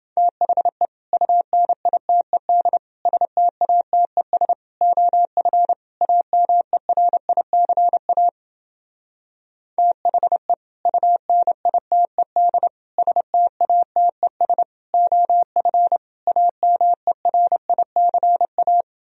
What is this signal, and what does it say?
Morse code, heard as a keyed tone, about 28 wpm, 700 Hz. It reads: THE UNITED STATES OF AMERICA THE UNITED STATES OF AMERICA